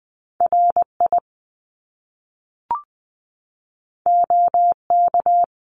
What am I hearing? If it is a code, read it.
Morse code, heard as a keyed tone, about 20 wpm, 700 Hz. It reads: RI OK
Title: Top Two Letter Combinations in Words 20wpm